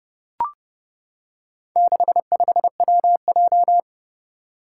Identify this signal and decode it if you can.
Morse code, heard as a keyed tone, about 29 wpm, 700 Hz. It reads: 65WJ
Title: Four Challenging Letter-Number Combinations 30wpm